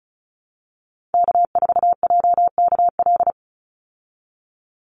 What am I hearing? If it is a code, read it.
Morse code, heard as a keyed tone, about 34 wpm, 700 Hz. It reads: K4JKL
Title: Call Signs Repeated 3x - 35wpm